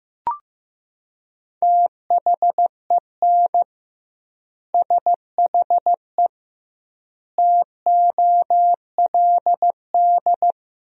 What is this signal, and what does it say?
Morse code, heard as a keyed tone, about 15 wpm, 700 Hz. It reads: THEN SHE TOLD
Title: 2x Word-Spacing - Sentences from Top 400 Words - 15wpm